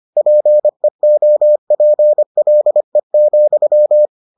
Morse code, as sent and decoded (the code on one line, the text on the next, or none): .--. --- .--. .-.. --..--
PEOPLE,